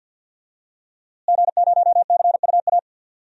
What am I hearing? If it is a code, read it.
Morse code, heard as a keyed tone, about 50 wpm, 700 Hz. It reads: G0CUA
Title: DXCC-100 Call Signs - 5 Characters - Repeated 3x - 50wpm